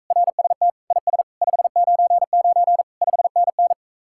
Morse code, heard as a keyed tone, about 46 wpm, 700 Hz. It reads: RST IS 599 5NN